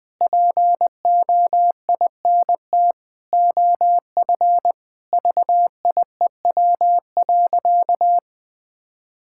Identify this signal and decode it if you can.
Morse code, heard as a keyed tone, about 20 wpm, 700 Hz. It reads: POINT OF VIEW.
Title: Sentences from Top 1000 Words - Review Part 6 - 20wpm